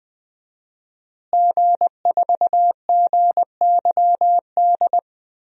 Morse code, heard as a keyed tone, about 20 wpm, 700 Hz. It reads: G4GYD
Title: DXCC-100 Call Signs - 5 Characters - Repeated 2x - 20wpm